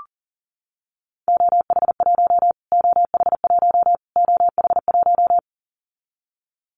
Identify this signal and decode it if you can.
Morse code, heard as a keyed tone, about 40 wpm, 700 Hz. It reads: OH1 OH1 OH1
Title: Callsign Prefixes for Top 25 DXCC - Repeated 3x - 40wpm